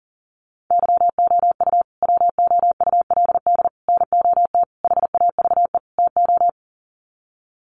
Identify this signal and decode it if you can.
Morse code, heard as a keyed tone, about 40 wpm, 700 Hz. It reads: YOU WOULD NOT HAVE TO
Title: Sets of 5 Words Encore 40wpm